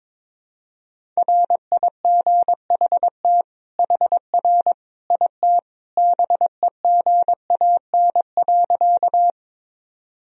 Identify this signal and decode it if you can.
Morse code, heard as a keyed tone, about 22 wpm, 700 Hz. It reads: RIGHT HR IT BEGAN.
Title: Sentences from Top 300 Words and Abbreviations 22wpm